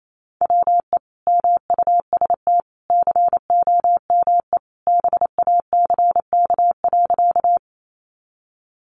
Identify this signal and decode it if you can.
Morse code, heard as a keyed tone, about 28 wpm, 700 Hz. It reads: WE MUST COME BACK.
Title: Sentences from Top 200 Words 28wpm